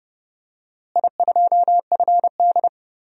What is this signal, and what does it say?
Morse code, heard as a keyed tone, about 30 wpm, 700 Hz. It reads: I2FD
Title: DXCC-100 Call Signs - 4 Characters - Repeated 3x - 30wpm